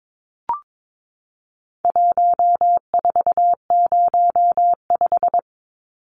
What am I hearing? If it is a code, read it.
Morse code, heard as a keyed tone, about 22 wpm, 700 Hz. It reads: 1405
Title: Numbers - 4-Digit 22wpm